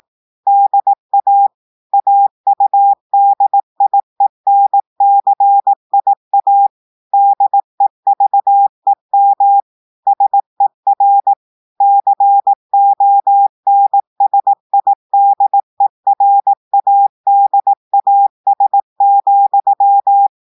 Morse code, heard as a keyed tone, about 18 wpm, 800 Hz. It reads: DA AUDIENCIA DEVEM SER CONSIDERADAS,
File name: audios / portugais / portugais texte en clair 2 vitesse 18 mots minutes